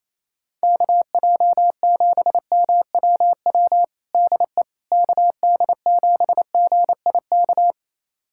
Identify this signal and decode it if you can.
Morse code, heard as a keyed tone, about 28 wpm, 700 Hz. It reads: KJ7MWW DE KD7GIK
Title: USA Callsign DE Callsign - 6 Characters - 28wpm